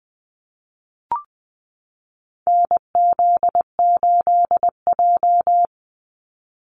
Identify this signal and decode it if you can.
Morse code, heard as a keyed tone, about 20 wpm, 700 Hz. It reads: NZ8J
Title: Call Signs - 20wpm